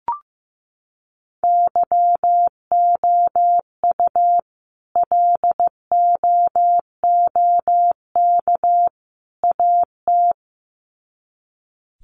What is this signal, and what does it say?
Morse code, heard as a keyed tone, about 15 wpm, 700 Hz. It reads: YOU LOOK AT
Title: Sets of 3 Words 15wpm